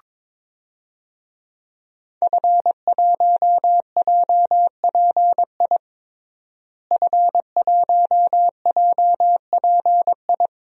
Morse code, heard as a keyed tone, about 22 wpm, 700 Hz. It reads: F1JPI F1JPI
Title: DXCC-100 Call Signs - 5 Characters - Repeated 2x - 22wpm